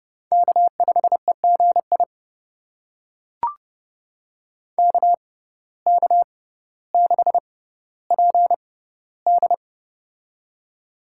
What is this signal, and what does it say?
Morse code, heard as a keyed tone, about 30 wpm, spaced as for 15 wpm, 700 Hz. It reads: K5EGI KK6PD
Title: USA Call Signs - 5 Characters - 6x Character Spacing - 30wpm